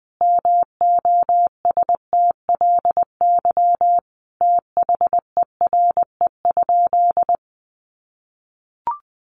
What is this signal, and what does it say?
Morse code, heard as a keyed tone, about 20 wpm, 700 Hz. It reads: MOSTLY THERE?
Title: Sentences from Top 1300 Words 20wpm